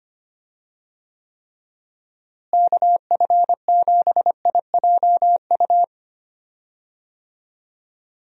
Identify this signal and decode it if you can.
Morse code, heard as a keyed tone, about 25 wpm, 700 Hz. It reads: KF7IJU
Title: USA Call Signs - 6 Characters - Repeated 3x - 25wpm